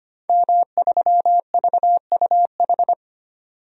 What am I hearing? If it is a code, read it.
Morse code, heard as a keyed tone, about 25 wpm, 700 Hz. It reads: M3VUH
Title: DXCC-100 Call Signs - 5 Characters - 25wpm